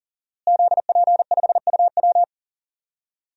Morse code, heard as a keyed tone, about 37 wpm, 700 Hz. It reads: ZP5UW
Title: DXCC-100 Call Signs - 5 Characters - Repeated 2x - 40wpm